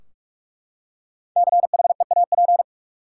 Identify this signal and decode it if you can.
Morse code, heard as a keyed tone, about 44 wpm, 700 Hz. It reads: CHEAP
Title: Five Letter Words - 45wpm